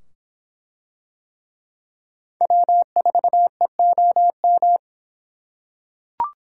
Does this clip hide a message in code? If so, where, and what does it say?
Yes: Morse code from 2.4 to 4.8 s, heard as a keyed tone, about 25 wpm, 700 Hz. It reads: W4EOM